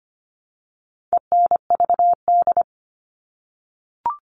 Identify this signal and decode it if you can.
Morse code, heard as a keyed tone, about 25 wpm, 700 Hz. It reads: ENVD
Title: All Cut Numbers - 4-Digit 25wpm